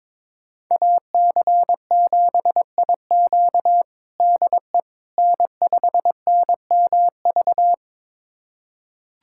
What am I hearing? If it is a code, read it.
Morse code, heard as a keyed tone, about 22 wpm, 700 Hz. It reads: AC7IQ DE N5NMV
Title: USA Callsign DE Callsign - 5 Characters - 22wpm